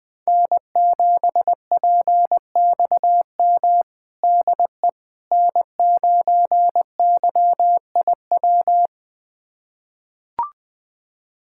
Morse code, heard as a keyed tone, about 20 wpm, 700 Hz. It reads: N7PXM DE N9YIW
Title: USA Callsign DE Callsign - 5 Characters - 20wpm